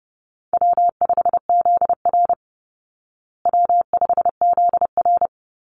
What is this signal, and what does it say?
Morse code, heard as a keyed tone, about 30 wpm, 700 Hz. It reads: W5ZR W5ZR
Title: Easy Call Signs Repeated 3x - 30wpm